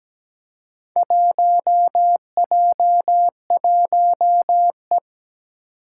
Morse code, heard as a keyed tone, about 17 wpm, 700 Hz. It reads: 1J1E
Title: Four Challenging Letter-Number Combinations 17wpm